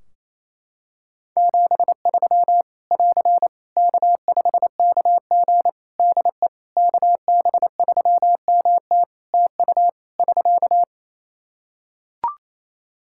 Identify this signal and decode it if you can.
Morse code, heard as a keyed tone, about 28 wpm, 700 Hz. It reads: 73 <AR> K5KG DE KB3MT TU <SK>
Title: K1USN SST QSO Practice 28wpm